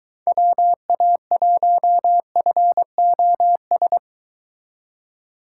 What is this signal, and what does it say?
Morse code, heard as a keyed tone, about 23 wpm, 700 Hz. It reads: WA1FOS